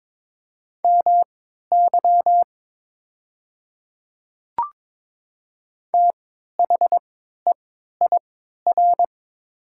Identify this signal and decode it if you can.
Morse code, heard as a keyed tone, about 22 wpm, spaced as for 12 wpm, 700 Hz. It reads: MY THEIR
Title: 3x Character Spacing - Top 100 Words - 22wpm